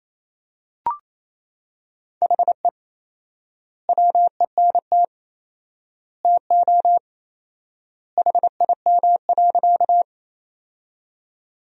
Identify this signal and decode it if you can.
Morse code, heard as a keyed tone, about 28 wpm, 700 Hz. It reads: HE WENT TO HIM.
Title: 4x Word-Spacing - Sentences from Top 200 Words - 28wpm